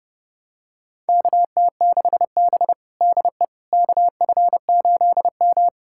Morse code, heard as a keyed tone, about 30 wpm, 700 Hz. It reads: KT6B DE KF8M
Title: USA Callsign DE Callsign - 4 Characters - 30wpm